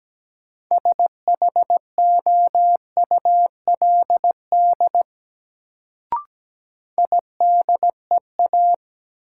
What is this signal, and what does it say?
Morse code, heard as a keyed tone, about 17 wpm, 700 Hz. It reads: SHOULD IDEA
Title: Sentences from Top 300 Words - New Word Drill - 17wpm